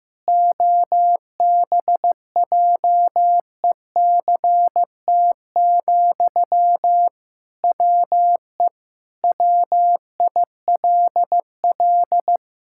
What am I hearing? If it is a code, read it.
Morse code, heard as a keyed tone, about 15 wpm, 700 Hz. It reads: OBJECT, WE WILL
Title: Sentences from Top 1000 Words - Review Part 6 - 15wpm